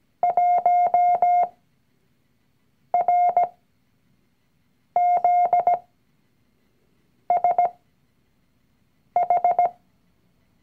Morse code, heard as a keyed tone, about 17 wpm, spaced as for 5 wpm, 710 Hz. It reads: 1RZSH
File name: CW-Online-Kurs / Anfänger / Lektion 019